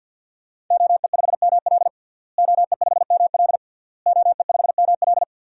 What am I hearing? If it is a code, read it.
Morse code, heard as a keyed tone, about 53 wpm, 700 Hz. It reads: OE5ML OE5ML OE5ML
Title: DXCC-100 Call Signs - 5 Characters - Repeated 3x - 50wpm